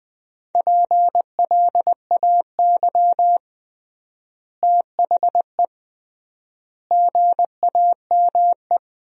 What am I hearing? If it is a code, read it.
Morse code, heard as a keyed tone, about 20 wpm, 700 Hz. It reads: PLAY THE GAME
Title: 3x Word-Spacing - Sentences from Top 500 Words - 20wpm